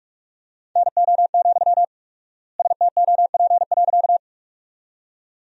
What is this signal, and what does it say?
Morse code, heard as a keyed tone, about 46 wpm, 700 Hz. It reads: NO, STOP.
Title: 4x Word-Spacing - Sentences from Top 300 Words - 45wpm